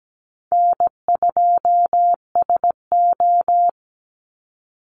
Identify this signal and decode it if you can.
Morse code, heard as a keyed tone, about 17 wpm, 700 Hz. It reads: N2SO
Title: Easy Call Signs Repeated 2x - 17wpm